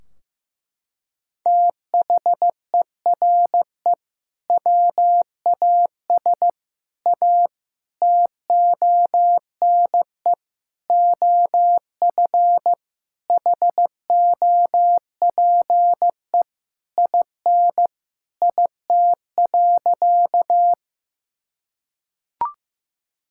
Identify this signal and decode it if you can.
Morse code, heard as a keyed tone, about 15 wpm, 700 Hz. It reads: THERE WAS A TONE OF HOPE IN IT.
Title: Sentences from Top 1600 Words 15wpm